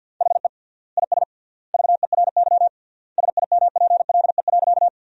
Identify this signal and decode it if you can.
Morse code, heard as a keyed tone, about 50 wpm, 700 Hz. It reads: HE IS VERY SIMPLE.
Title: 3x Word-Spacing - Sentences from Top 400 Words - 50wpm